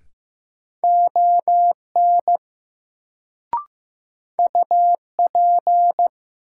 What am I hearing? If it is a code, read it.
Morse code, heard as a keyed tone, about 15 wpm, 700 Hz. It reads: ON UP
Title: Sets of 2 Words - New Word Drill - 15wpm